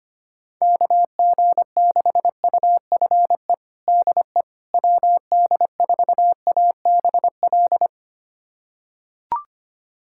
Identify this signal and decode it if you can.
Morse code, heard as a keyed tone, about 25 wpm, 700 Hz. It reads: KG6UFE DE WD4ABL
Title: USA Callsign DE Callsign 25wpm